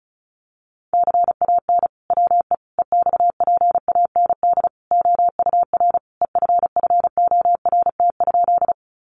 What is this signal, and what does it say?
Morse code, heard as a keyed tone, about 34 wpm, 700 Hz. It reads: CAN WE EXPAND OUR EFFORT?
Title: Sentences from Top 1400 Words 35wpm